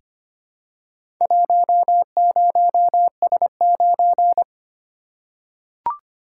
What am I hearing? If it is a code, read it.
Morse code, heard as a keyed tone, about 25 wpm, 690 Hz. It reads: 10S9
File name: practice / 25wpm / Four Letter-Number Combinations - 25wpm